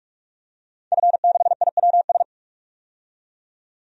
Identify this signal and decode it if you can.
Morse code, heard as a keyed tone, about 46 wpm, 700 Hz. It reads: F6IWS
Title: DXCC-100 Call Signs - 5 Characters - 45wpm